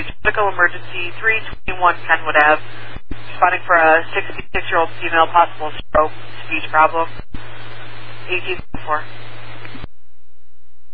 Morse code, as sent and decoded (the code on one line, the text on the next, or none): none